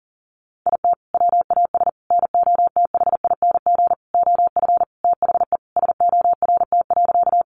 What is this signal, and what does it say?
Morse code, heard as a keyed tone, about 40 wpm, 700 Hz. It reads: IT WAS NOTHING OF THE SORT.
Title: Sentences from Top 1300 Words 40wpm